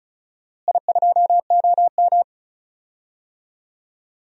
Morse code, heard as a keyed tone, about 34 wpm, 700 Hz. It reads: I2OM
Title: DXCC-100 Call Signs - 4 Characters - Repeated 3x - 35wpm